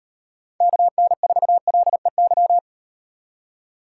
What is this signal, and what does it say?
Morse code, heard as a keyed tone, about 38 wpm, 700 Hz. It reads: KN4LEY